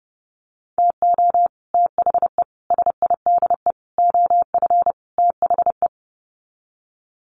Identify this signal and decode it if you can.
Morse code, heard as a keyed tone, about 30 wpm, 700 Hz. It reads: TO THE SIDE OF THE
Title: Sets of 5 Words 30wpm